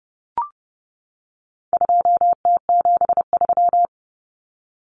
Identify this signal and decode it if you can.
Morse code, heard as a keyed tone, about 30 wpm, 700 Hz. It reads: 2T73
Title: Common Cut Numbers - 4-Digit 30wpm